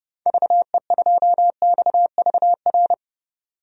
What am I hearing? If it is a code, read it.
Morse code, heard as a keyed tone, about 30 wpm, 700 Hz. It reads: VE2XVR
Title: DXCC-100 Call Signs - 6 Characters - 30wpm